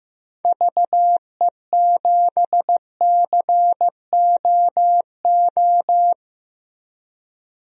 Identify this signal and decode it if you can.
Morse code, heard as a keyed tone, about 15 wpm, 700 Hz. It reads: VE7COO
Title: DXCC-100 Call Signs - 6 Characters - Repeated 3x - 15wpm